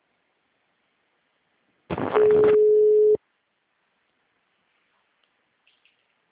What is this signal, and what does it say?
Ringback tone (Europe, 425 Hz).